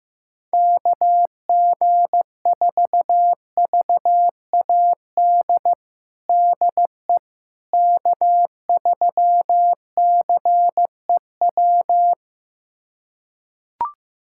Morse code, heard as a keyed tone, about 15 wpm, 700 Hz. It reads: KG4VAD DE K3CEW
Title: USA Callsign DE Callsign 15wpm